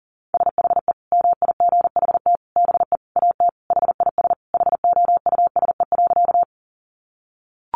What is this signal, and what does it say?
Morse code, heard as a keyed tone, about 40 wpm, 700 Hz. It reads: SHE MIGHT BE AT HIS HOUSE.
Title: Sentences from Top 300 Words 40wpm